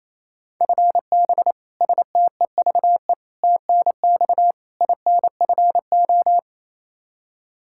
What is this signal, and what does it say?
Morse code, heard as a keyed tone, about 28 wpm, 700 Hz. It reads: FB STEVE TNX INFO
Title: Practice QSO - Exchange 3 - More Info - 28wpm